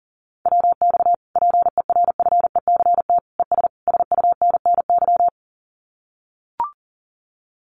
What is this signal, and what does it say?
Morse code, heard as a keyed tone, about 40 wpm, 700 Hz. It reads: WX PERFECT ES SUNNY
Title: Common QSO Phrases Encore 40wpm